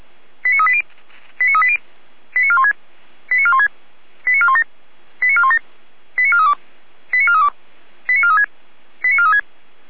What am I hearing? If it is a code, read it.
Five-tone selective call (ZVEI), heard as quick runs of five tones, tones of about 70 ms. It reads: 89380 89380 89416 89416 89417 89417 89432 89432 89436 89436
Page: SDR/ZVEI